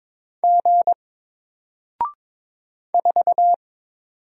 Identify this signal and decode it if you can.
Morse code, heard as a keyed tone, about 22 wpm, 700 Hz. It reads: G 4
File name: Single-Letter-Number-22wpm